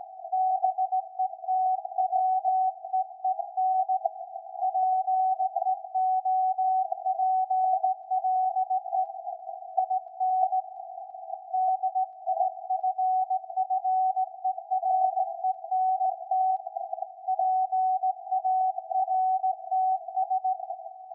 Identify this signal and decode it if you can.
Morse code, heard as a keyed tone, about 17 wpm, 750 Hz. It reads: BETWEEN PEOPLE IN DIFFERENT PARTS
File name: iz7slz / VLF / SAQ 13-02-2023